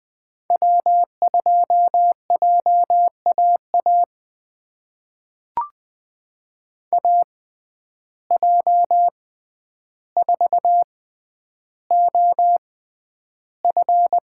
Morse code, heard as a keyed tone, about 20 wpm, spaced as for 10 wpm, 700 Hz. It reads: W2JAA AJ4OF
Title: USA Call Signs - 5 Characters - 6x Character Spacing - 20wpm